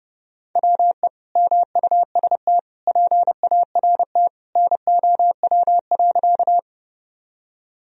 Morse code, heard as a keyed tone, about 30 wpm, 700 Hz. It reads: WE MUST PART NOW.
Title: Sentences from Top 1000 Words - Review Part 1 - 30wpm